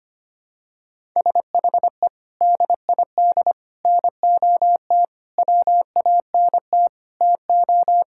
Morse code, heard as a keyed tone, about 25 wpm, 700 Hz. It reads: SHE DID NOT WANT TO
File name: practice / 25wpm / Sentences with 4-Word Phrases - 25wpm